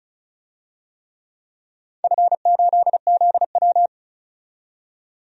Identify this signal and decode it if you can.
Morse code, heard as a keyed tone, about 36 wpm, 700 Hz. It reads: F8ZW
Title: DXCC-100 Call Signs - 4 Characters - 35wpm